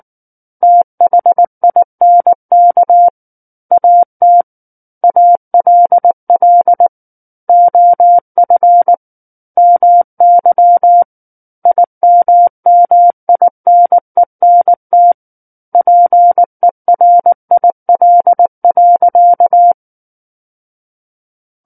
Morse code, heard as a keyed tone, about 19 wpm, 700 Hz. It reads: THINK AT ALL OF MY IMMINENT PERIL.
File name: Book One / 10-Chapter 10 - 19 WPM